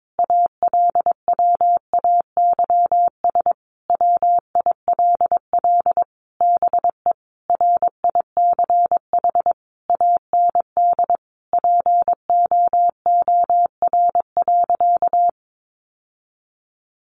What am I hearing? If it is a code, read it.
Morse code, heard as a keyed tone, about 22 wpm, 700 Hz. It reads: ALWAYS WILL BE RICH AND POOR.
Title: Sentences from Top 1100 Words 22wpm